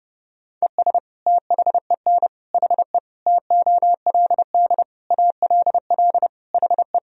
Morse code, heard as a keyed tone, about 30 wpm, 700 Hz. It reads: ES THEN HE TOLD ALL HE
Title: Sentences from Top 400 Words and Abbreviations 30wpm